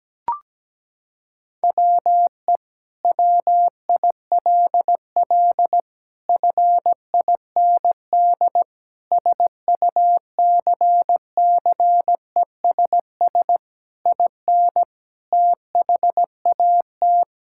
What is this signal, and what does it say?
Morse code, heard as a keyed tone, about 17 wpm, 700 Hz. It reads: WE WILL FIND SUCCESS IN THAT
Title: Sentences from Top 1000 Words - Review Part 5 - 17wpm